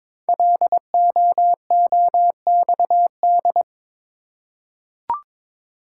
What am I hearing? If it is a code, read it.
Morse code, heard as a keyed tone, about 22 wpm, 700 Hz. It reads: LOOXD
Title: Code Groups - 5 Letter:Numbers - 22wpm